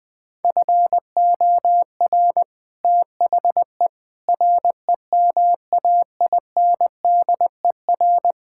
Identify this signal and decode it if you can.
Morse code, heard as a keyed tone, about 20 wpm, 700 Hz. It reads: FOR THE REMAINDER
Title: Sets of 6 Words - 20wpm